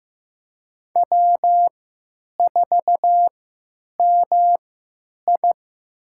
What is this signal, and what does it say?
Morse code, heard as a keyed tone, about 15 wpm, spaced as for 8 wpm, 700 Hz. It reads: W4MI
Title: USA Call Signs - 4 Characters - 3x Character Spacing - 15wpm